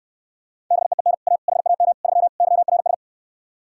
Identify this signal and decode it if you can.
Morse code, heard as a keyed tone, about 69 wpm, 700 Hz. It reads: BEEN A HAM 3 YRS